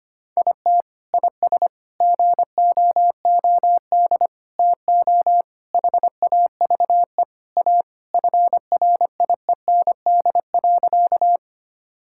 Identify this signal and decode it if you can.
Morse code, heard as a keyed tone, about 25 wpm, 700 Hz. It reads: IT IS GOOD TO HAVE A FRIEND.
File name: practice / 25wpm / Sentences with 3-Word Phrases - 25wpm